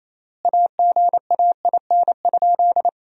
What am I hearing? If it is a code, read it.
Morse code, heard as a keyed tone, about 28 wpm, 700 Hz. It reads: AGAIN?
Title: 3x Word-Spacing - Sentences from Top 200 Words - 28wpm